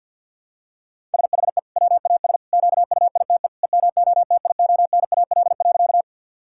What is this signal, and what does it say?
Morse code, heard as a keyed tone, about 47 wpm, 700 Hz. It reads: SHE WAS QUITE EMOTIONAL.
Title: Sentences from Top 1000 Words - Review Part 5 - 50wpm